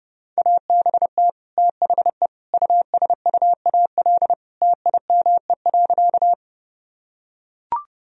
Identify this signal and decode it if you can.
Morse code, heard as a keyed tone, about 32 wpm, 700 Hz. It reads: ABT THE USUAL TIME.
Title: Sentences from Top 400 Words and Abbreviations 30wpm